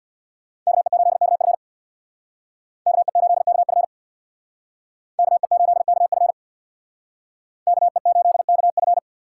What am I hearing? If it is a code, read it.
Morse code, heard as a keyed tone, about 63 wpm, 700 Hz. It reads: KE8KF KE8KF KE8KF KE8KF